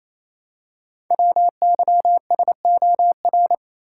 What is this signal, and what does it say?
Morse code, heard as a keyed tone, about 28 wpm, 700 Hz. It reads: WYSOR
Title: Code Groups - 5 Letters - 28wpm